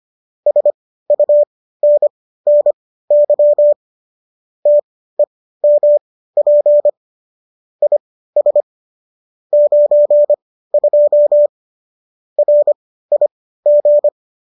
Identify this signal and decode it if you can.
Morse code, heard as a keyed tone, about 25 wpm, spaced as for 15 wpm, 590 Hz. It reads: SUNNY TEMP IS 92 RIG